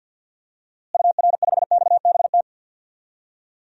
Morse code, heard as a keyed tone, about 52 wpm, 700 Hz. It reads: UR5XBT
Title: DXCC-100 Call Signs - 6 Characters - Repeated 3x - 50wpm